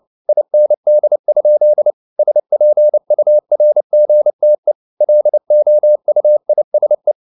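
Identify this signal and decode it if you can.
Morse code, heard as a keyed tone, about 29 wpm, 600 Hz. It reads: IND? SPURGTE LOUISE
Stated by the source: Fast (QRQ) Morse at 29 wpm, from CW med Gnister.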